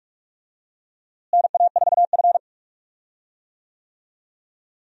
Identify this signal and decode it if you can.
Morse code, heard as a keyed tone, about 46 wpm, 700 Hz. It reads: NA4F
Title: USA Call Signs - 4 Characters - Repeated 2x - 45wpm